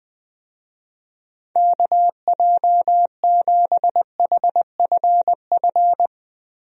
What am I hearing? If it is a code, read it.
Morse code, heard as a keyed tone, about 21 wpm, 700 Hz. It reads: KJ7HFF
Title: USA Call Signs - 6 Characters - Repeated 3x - 20wpm